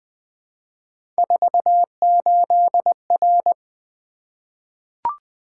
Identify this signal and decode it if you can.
Morse code, heard as a keyed tone, about 20 wpm, 700 Hz. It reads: 48R
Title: Three Letter-Number Combinations - 20wpm